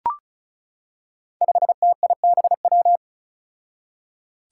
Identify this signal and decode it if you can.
Morse code, heard as a keyed tone, about 34 wpm, 700 Hz. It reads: 5TIBW
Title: Five Challenging Letter-Number Combinations 35wpm